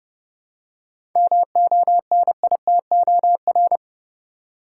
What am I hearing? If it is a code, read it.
Morse code, heard as a keyed tone, about 31 wpm, 700 Hz. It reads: MONITOR
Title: Seven Letter Words - 30wpm